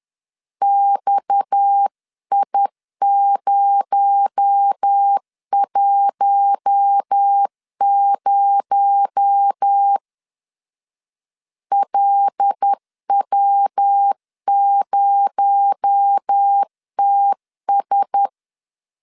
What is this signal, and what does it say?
Morse code, heard as a keyed tone, about 10 wpm, 800 Hz. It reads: XI010 LW0TS
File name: LEZIONI / 6 PKX0 / 6 PKX0 10wpm 2707